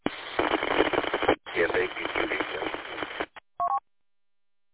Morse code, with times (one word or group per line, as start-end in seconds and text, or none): none